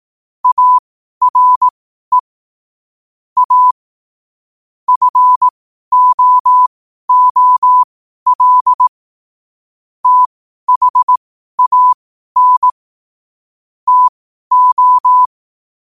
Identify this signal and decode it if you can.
Morse code, heard as a keyed tone, about 18 wpm, spaced as for 12 wpm, 1000 Hz. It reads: ARE A FOOL THAN TO
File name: qotdfiles / 12wpm